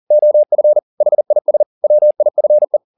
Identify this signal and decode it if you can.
Morse code, heard as a keyed tone, about 40 wpm, 600 Hz. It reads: OF HIS WIFE